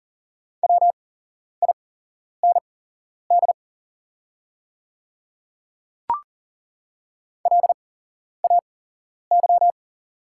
Morse code, heard as a keyed tone, about 40 wpm, spaced as for 11 wpm, 690 Hz. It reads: WIND LAY
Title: 8x Character Spacing - Top 500 Words - 40wpm